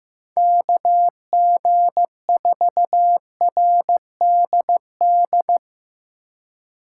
Morse code, heard as a keyed tone, about 15 wpm, 700 Hz. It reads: KG4RDD
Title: USA Call Signs - 6 Characters - 3x Character Spacing - 15wpm